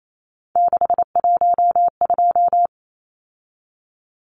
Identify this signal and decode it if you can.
Morse code, heard as a keyed tone, about 28 wpm, 700 Hz. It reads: 612
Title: Numbers - 3-Digit 28wpm